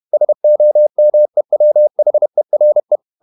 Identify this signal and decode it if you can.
Morse code, heard as a keyed tone, about 31 wpm, 600 Hz. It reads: SOMEWHERE
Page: Never 31wpm | CW med Gnister